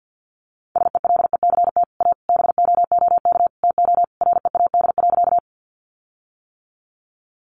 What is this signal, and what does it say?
Morse code, heard as a keyed tone, about 50 wpm, 700 Hz. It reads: SELECT A BOOK TO READ.
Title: Sentences from Top 1300 Words 50wpm